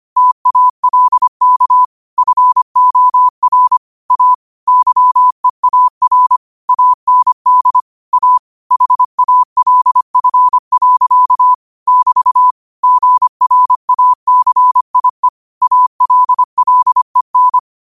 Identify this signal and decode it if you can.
Morse code, heard as a keyed tone, about 25 wpm, 1000 Hz. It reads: TALK FOR A YEAR AND A HALF. <BT> GRACIE ALLEN
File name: 25wpm